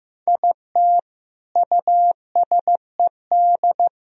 Morse code, heard as a keyed tone, about 17 wpm, 700 Hz. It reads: IT USED
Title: Speed-Racing - Sets of 4 Words 15wpm